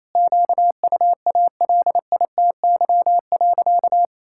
Morse code, heard as a keyed tone, about 28 wpm, 700 Hz. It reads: QUALITY.
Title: Sentences from Top 1000 Words - Review Part 5 - 28wpm